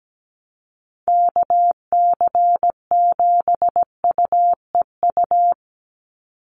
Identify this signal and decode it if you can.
Morse code, heard as a keyed tone, about 17 wpm, 700 Hz. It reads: KC7UEU
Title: USA Call Signs 17wpm